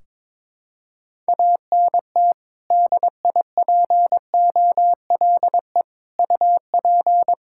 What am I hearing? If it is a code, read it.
Morse code, heard as a keyed tone, about 22 wpm, 700 Hz. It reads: ANT DIPOLE UP
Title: Practice QSO - Exchange 2 - Info - 22wpm